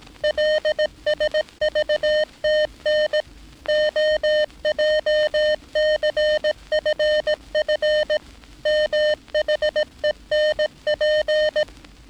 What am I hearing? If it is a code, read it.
Morse code, heard as a keyed tone, about 18 wpm, 610 Hz. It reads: LSVTN OJCFF MHENP
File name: training / NAVPERS 11049to11060 / NAVPERS 11059 RA RB